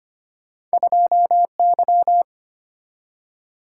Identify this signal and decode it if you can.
Morse code, heard as a keyed tone, about 34 wpm, spaced as for 30 wpm, 700 Hz. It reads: 2Y
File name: practice / 25wpm / Speed-Racing - Two Letter-Number Combinations - 25wpm